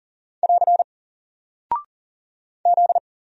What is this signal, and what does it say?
Morse code, heard as a keyed tone, about 40 wpm, 700 Hz. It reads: <AR> Z
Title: Single Character (Letter:Number:Punctuation:Prosign) Alt 40wpm